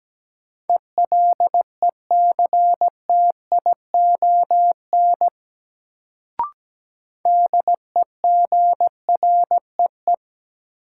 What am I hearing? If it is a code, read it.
Morse code, heard as a keyed tone, about 17 wpm, 700 Hz. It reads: ELECTION DEGREE
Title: Sentences from Top 1500 Words - New Word Drill Alt - 17wpm